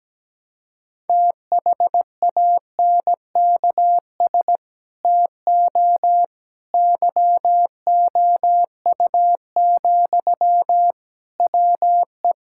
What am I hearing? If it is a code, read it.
Morse code, heard as a keyed tone, about 17 wpm, 700 Hz. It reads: THANKS TO YOU, WE
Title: Sentences from Top 1000 Words - Review Part 5 - 17wpm